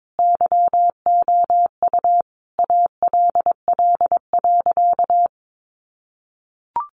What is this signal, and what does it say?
Morse code, heard as a keyed tone, about 22 wpm, 700 Hz. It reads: YOU ALL.
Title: Sentences from Top 1100 Words 22wpm